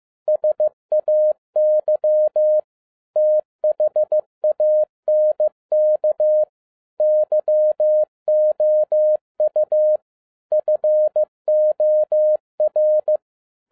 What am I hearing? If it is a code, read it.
Morse code, heard as a keyed tone, about 15 wpm, 610 Hz. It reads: SAY THANK YOU FOR
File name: 2011-07-mp3 / n37 THE 3.11 2 15wpm